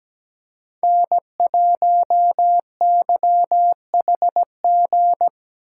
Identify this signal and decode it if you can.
Morse code, heard as a keyed tone, about 17 wpm, 700 Hz. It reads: N1YHG
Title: USA Call Signs - 5 Characters - 3x Character Spacing - 17wpm